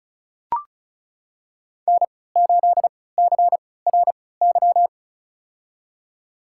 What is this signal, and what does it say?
Morse code, heard as a keyed tone, about 34 wpm, spaced as for 20 wpm, 700 Hz. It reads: N8CRY
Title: USA Call Signs - 5 Characters - 3x Character Spacing - 35wpm